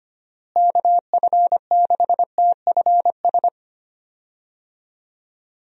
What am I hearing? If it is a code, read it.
Morse code, heard as a keyed tone, about 25 wpm, 700 Hz. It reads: KF6TFS
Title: USA Call Signs - 6 Characters - Repeated 2x - 25wpm